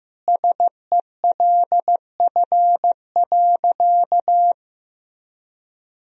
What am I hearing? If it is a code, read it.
Morse code, heard as a keyed tone, about 16 wpm, 700 Hz. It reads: SELF.
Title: Sentences from Top 1000 Words - Review Part 1 - 15wpm